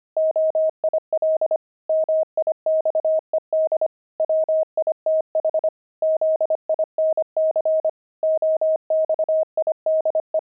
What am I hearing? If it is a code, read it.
Morse code, heard as a keyed tone, about 25 wpm, 630 Hz. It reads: OIL MIXED WITH ZINC OXIDE
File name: pangram-fs-25-cs-25-oxide